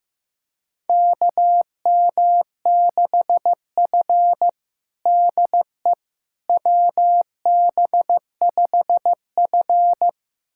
Morse code, heard as a keyed tone, about 15 wpm, 700 Hz. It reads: KM6F DE WB5F
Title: USA Callsign DE Callsign - 4 Characters - 15wpm